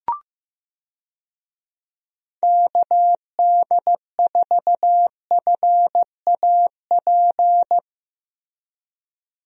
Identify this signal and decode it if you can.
Morse code, heard as a keyed tone, about 15 wpm, 700 Hz. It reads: KD4FAP
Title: USA Call Signs - 6 Characters - Repeated 2x - 15wpm